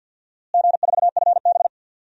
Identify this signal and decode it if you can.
Morse code, heard as a keyed tone, about 52 wpm, 700 Hz. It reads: G4FB
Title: DXCC-100 Call Signs - 4 Characters - Repeated 3x - 50wpm